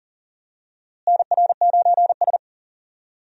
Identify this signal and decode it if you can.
Morse code, heard as a keyed tone, about 38 wpm, 700 Hz. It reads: NR9S
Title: USA Call Signs - 4 Characters - Repeated 2x - 40wpm